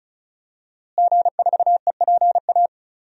Morse code, heard as a keyed tone, about 36 wpm, 700 Hz. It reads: G4EPA
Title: DXCC-100 Call Signs - 5 Characters - Repeated 2x - 35wpm